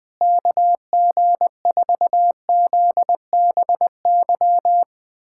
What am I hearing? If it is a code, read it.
Morse code, heard as a keyed tone, about 20 wpm, 700 Hz. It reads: KG4ZBY
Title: USA Callsign DE Callsign 20wpm